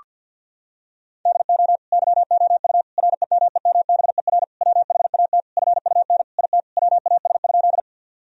Morse code, heard as a keyed tone, about 52 wpm, 700 Hz. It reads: DO YOU REMEMBER WHAT FUN IT WAS?
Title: Sentences from Top 1000 Words - Review Part 4 - 50wpm